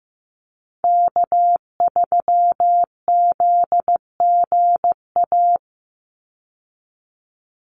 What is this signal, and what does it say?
Morse code, heard as a keyed tone, about 15 wpm, 700 Hz. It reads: K3ZGA
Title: Call Signs Repeated 3x - 15wpm